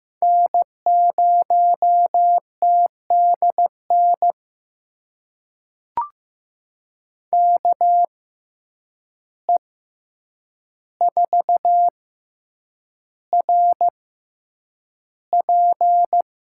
Morse code, heard as a keyed tone, about 15 wpm, spaced as for 8 wpm, 700 Hz. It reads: N0TDN KE4RP
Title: USA Call Signs - 6x Character Spacing - 15wpm